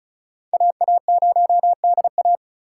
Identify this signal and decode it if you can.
Morse code, heard as a keyed tone, about 36 wpm, 700 Hz. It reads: AA0DA
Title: USA Call Signs - 5 Characters - 3x Character Spacing - 35wpm